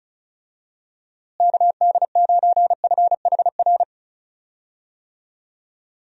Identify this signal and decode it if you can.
Morse code, heard as a keyed tone, about 36 wpm, 700 Hz. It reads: KD9FHR
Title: USA Call Signs - 6 Characters - Repeated 2x - 35wpm